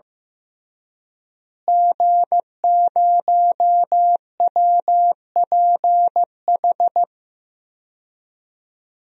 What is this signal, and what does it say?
Morse code, heard as a keyed tone, about 15 wpm, 700 Hz. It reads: G0WPH
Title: DXCC-100 Call Signs - 5 Characters - Repeated 3x - 15wpm